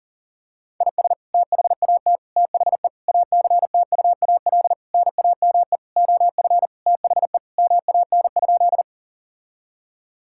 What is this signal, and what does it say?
Morse code, heard as a keyed tone, about 40 wpm, 700 Hz. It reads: IS THAT THE ACTUAL NAME OF THE MAN?
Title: Sentences from Top 1000 Words - Review Part 5 - 40wpm